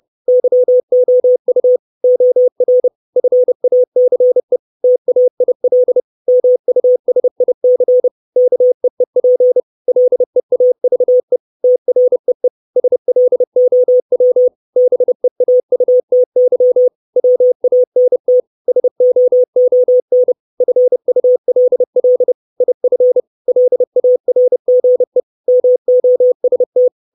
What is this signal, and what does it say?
Morse code, heard as a keyed tone, about 30 wpm, 500 Hz. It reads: YOU OR FACE TAIL MUSIC KEEP LEAVE TREE SLOW BEAUTY WANT SOON FULL IF LARGE MOST
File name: 632 Words 30WPM